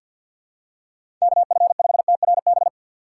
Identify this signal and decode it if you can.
Morse code, heard as a keyed tone, about 49 wpm, 700 Hz. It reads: KF5TRB
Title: USA Call Signs - 6 Characters - 3x Character Spacing - 50wpm